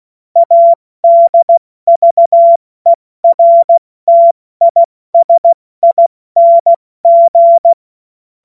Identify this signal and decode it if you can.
Morse code, heard as a keyed tone, about 16 wpm, spaced as for 14 wpm, 680 Hz. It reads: ADVERTISING
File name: Morsekurs / Witze / Witz 27. September 2022 70 F80